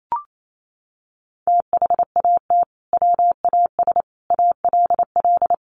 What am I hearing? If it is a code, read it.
Morse code, heard as a keyed tone, about 28 wpm, 700 Hz. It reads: THAT WAS ALL
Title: Sentences from Top 100 Words 28wpm